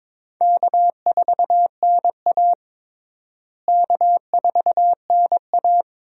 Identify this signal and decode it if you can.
Morse code, heard as a keyed tone, about 22 wpm, 700 Hz. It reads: K4NA K4NA
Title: USA Call Signs - 4 Characters - Repeated 2x - 22wpm